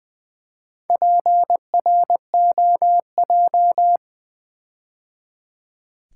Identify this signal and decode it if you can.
Morse code, heard as a keyed tone, about 20 wpm, 700 Hz. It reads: PROJ
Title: Build A Word - 4 Letters At A Time - 10-Letter Words - 20wpm